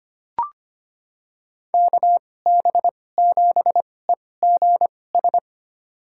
Morse code, heard as a keyed tone, about 25 wpm, spaced as for 18 wpm, 700 Hz. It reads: KB7EGS
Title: USA Call Signs - 2x Character Spacing - 25wpm